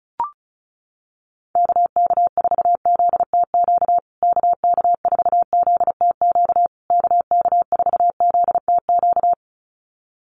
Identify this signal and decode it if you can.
Morse code, heard as a keyed tone, about 34 wpm, 700 Hz. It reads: KK4ZTQ KK4ZTQ KK4ZTQ
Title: USA Call Signs Repeated 3x 35wpm